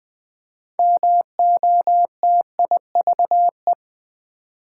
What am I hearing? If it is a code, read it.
Morse code, heard as a keyed tone, about 20 wpm, 700 Hz. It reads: MOTIVE
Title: 6 Letter Words - First 3 Letters Given - 20wpm